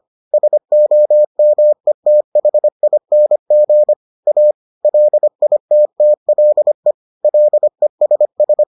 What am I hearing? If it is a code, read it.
Morse code, heard as a keyed tone, about 25 wpm, 600 Hz. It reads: SOMETHING A LITTLE LESS